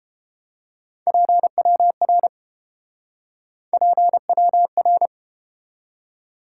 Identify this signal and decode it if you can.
Morse code, heard as a keyed tone, about 30 wpm, 700 Hz. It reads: PWR PWR